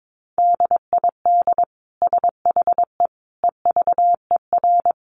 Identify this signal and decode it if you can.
Morse code, heard as a keyed tone, about 22 wpm, 700 Hz. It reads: DID SHE EVER
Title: Sentences from Top 1100 Words 22wpm